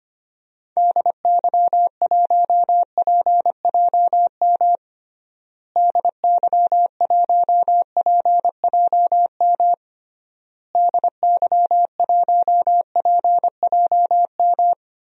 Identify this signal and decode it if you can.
Morse code, heard as a keyed tone, about 25 wpm, 700 Hz. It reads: DY1PJM DY1PJM DY1PJM
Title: DXCC-100 Call Signs - 6 Characters - Repeated 3x - 25wpm